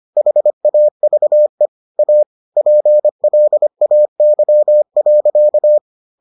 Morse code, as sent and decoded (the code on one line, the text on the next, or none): .... .- ...- . / .- / .--. .-.. .- -.-- .-.-.-
HAVE A PLAY.